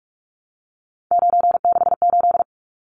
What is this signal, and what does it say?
Morse code, heard as a keyed tone, about 44 wpm, 700 Hz. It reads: 968
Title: Numbers - 3-Digit 45wpm